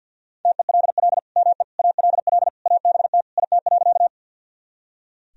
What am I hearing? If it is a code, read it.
Morse code, heard as a keyed tone, about 45 wpm, 700 Hz. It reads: TELL ME ALL ABT IT.
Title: Sentences from Top 200 Words and Abbreviations 50wpm